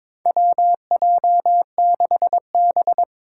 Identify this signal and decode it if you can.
Morse code, heard as a keyed tone, about 22 wpm, 700 Hz. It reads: WJ6B
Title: Four Challenging Letter-Number Combinations 22wpm